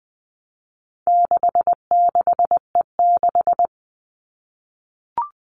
Morse code, heard as a keyed tone, about 20 wpm, 700 Hz. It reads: 66E6
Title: All Cut Numbers - 4-Digit 20wpm